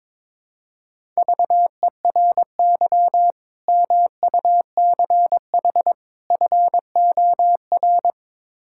Morse code, heard as a keyed tone, about 23 wpm, 700 Hz. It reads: VERY MUCH FOR
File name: practice / 22wpm / Build A Phrase - 5-Word Phrases - 22wpm